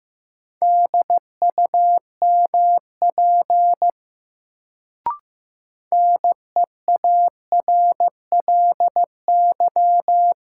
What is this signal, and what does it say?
Morse code, heard as a keyed tone, about 15 wpm, 700 Hz. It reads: DUMP NEARLY
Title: Sentences from Top 1300 Words - New Word Drill - 15wpm